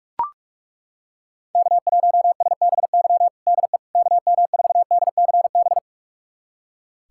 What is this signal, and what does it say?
Morse code, heard as a keyed tone, about 41 wpm, 700 Hz. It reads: K1SBY DE KM4DCB
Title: USA Callsign DE Callsign 45wpm